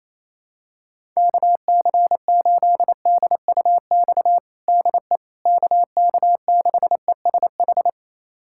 Morse code, heard as a keyed tone, about 27 wpm, 700 Hz. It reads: KC8DUX DE KK6ESH
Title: USA Callsign DE Callsign 28wpm